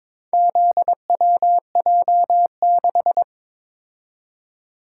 Morse code, heard as a keyed tone, about 22 wpm, 700 Hz. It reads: ZWJ6
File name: practice / 22wpm / Four Letter-Number Combinations - 22wpm